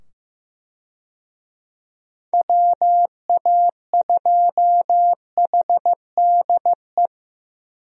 Morse code, heard as a keyed tone, about 15 wpm, 700 Hz. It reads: WA2HDE
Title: USA Call Signs - 6 Characters - 15wpm